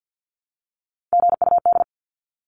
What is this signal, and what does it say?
Morse code, heard as a keyed tone, about 50 wpm, 700 Hz. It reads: GUD
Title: Abbreviations 50wpm